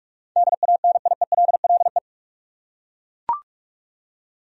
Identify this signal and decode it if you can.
Morse code, heard as a keyed tone, about 41 wpm, 700 Hz. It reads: DANIELLE
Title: USA Common Names - 45wpm